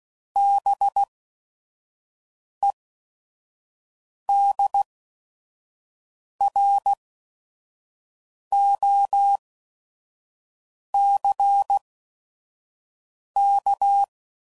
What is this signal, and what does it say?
Morse code, heard as a keyed tone, about 16 wpm, spaced as for 5 wpm, 800 Hz. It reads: BEDROCK
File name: CW Code Groups @5WPM / fromSM's / Cities